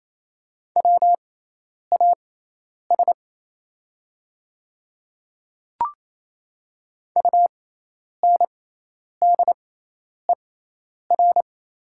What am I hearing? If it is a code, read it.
Morse code, heard as a keyed tone, about 28 wpm, spaced as for 10 wpm, 700 Hz. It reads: WAS UNDER
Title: 6x Character Spacing - Top 200 Words - 28wpm